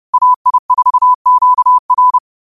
Morse code, heard as a keyed tone, about 30 wpm, 1000 Hz. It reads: AI4QR
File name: qotdfiles / 30wpm